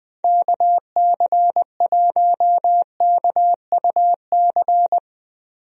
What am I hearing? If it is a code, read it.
Morse code, heard as a keyed tone, about 20 wpm, 700 Hz. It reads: KC1KUC
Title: USA Call Signs - 6 Characters - Repeated 3x - 20wpm